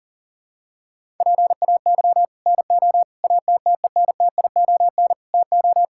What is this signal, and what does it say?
Morse code, heard as a keyed tone, about 40 wpm, 700 Hz. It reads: PAY NO ATTENTION TO
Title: Sentences from Top 1000 Words - Review Part 6 - 40wpm